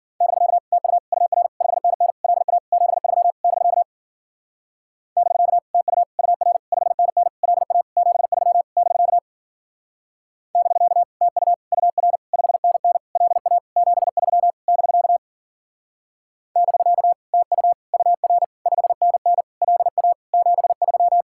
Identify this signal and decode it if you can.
Morse code, heard as a keyed tone, about 50 wpm, 700 Hz. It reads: <BK> TU UR 5NN LA 73 <BK> <BK> TU UR 5NN LA 73 <BK> <BK> TU UR 5NN LA 73 <BK> <BK> TU UR 5NN LA 73
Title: Speed-Racing Common QSO Phrases Encore - 40wpm